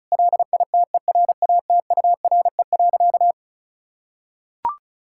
Morse code, heard as a keyed tone, about 36 wpm, 700 Hz. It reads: LITERATURE.
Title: Sentences from Top 1000 Words - Review Part 6 - 35wpm